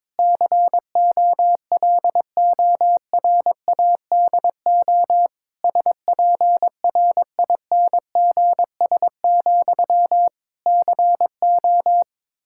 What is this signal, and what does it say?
Morse code, heard as a keyed tone, about 22 wpm, 700 Hz. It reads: COLORADO SPRINGS, CO
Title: Practice QSO - Exchange 1 - Beginning - 22wpm